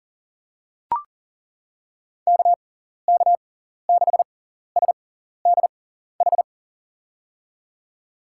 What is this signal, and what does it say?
Morse code, heard as a keyed tone, about 40 wpm, spaced as for 14 wpm, 700 Hz. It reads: KK6SDH
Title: USA Call Signs - 6 Characters - 6x Character Spacing - 40wpm